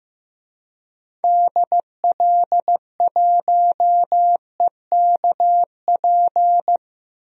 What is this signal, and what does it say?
Morse code, heard as a keyed tone, about 15 wpm, 700 Hz. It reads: DL1EKP
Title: DXCC-100 Call Signs - 6 Characters - Repeated 2x - 15wpm